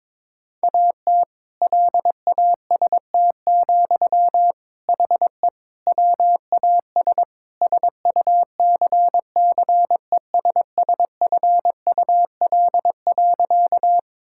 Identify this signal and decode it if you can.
Morse code, heard as a keyed tone, about 22 wpm, 700 Hz. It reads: AT LAST, HE WAS SUCCESSFUL.
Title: Sentences from Top 1000 Words - Review Part 6 - 22wpm